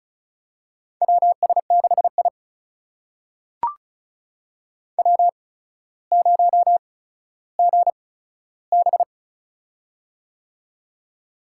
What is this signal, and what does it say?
Morse code, heard as a keyed tone, about 36 wpm, spaced as for 15 wpm, 700 Hz. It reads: WS6I W0GB
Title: USA Call Signs - 4 Characters - 8x Character Spacing - 35wpm